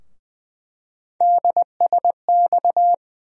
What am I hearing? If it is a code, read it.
Morse code, heard as a keyed tone, about 20 wpm, 700 Hz. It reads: DSX
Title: USA Call Sign Postfixes - 3 Letters 20wpm